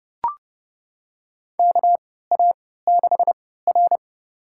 Morse code, heard as a keyed tone, about 32 wpm, spaced as for 17 wpm, 700 Hz. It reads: KA6R
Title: USA Call Signs - 3x Character Spacing - 30wpm